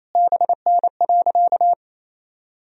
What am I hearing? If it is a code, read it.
Morse code, heard as a keyed tone, about 28 wpm, 700 Hz. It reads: BN.
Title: Sentences from Top 300 Words and Abbreviations 28wpm